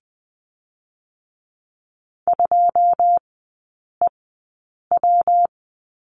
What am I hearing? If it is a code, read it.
Morse code, heard as a keyed tone, about 20 wpm, spaced as for 8 wpm, 700 Hz. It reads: 2EW
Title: 145 Lesson 24e - Letter y - Three Characters 20wpm